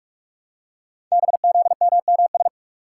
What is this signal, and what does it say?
Morse code, heard as a keyed tone, about 44 wpm, 700 Hz. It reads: BZMMS
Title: Code Groups - 5 Letters - 45wpm